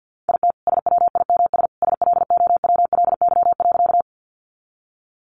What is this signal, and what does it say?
Morse code, heard as a keyed tone, about 50 wpm, 700 Hz. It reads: IT SWIMS SLOWLY.
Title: Additional Easy Phrases 50wpm